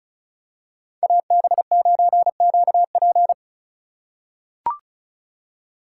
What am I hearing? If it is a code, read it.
Morse code, heard as a keyed tone, about 34 wpm, 700 Hz. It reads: AB9QP
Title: USA Call Signs - 5 Characters - 3x Character Spacing - 35wpm